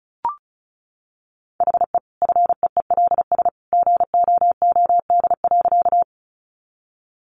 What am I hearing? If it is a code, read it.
Morse code, heard as a keyed tone, about 34 wpm, 700 Hz. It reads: HE FEELS GOOD.
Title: Additional Easy Phrases 35wpm